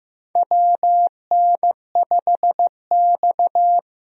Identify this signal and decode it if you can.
Morse code, heard as a keyed tone, about 15 wpm, 700 Hz. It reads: WN5X
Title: USA Call Signs - 4 Characters - Repeated 3x - 15wpm